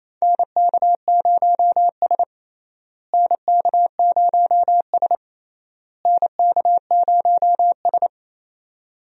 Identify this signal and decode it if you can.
Morse code, heard as a keyed tone, about 28 wpm, 700 Hz. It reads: NK0S NK0S NK0S
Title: USA Call Signs - 4 Characters - Repeated 3x - 28wpm